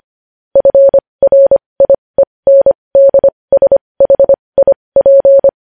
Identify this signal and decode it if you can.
Morse code, heard as a keyed tone, about 25 wpm, spaced as for 20 wpm, 550 Hz. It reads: FRIENDSHIP
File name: podcast / News headlines-1x-25x20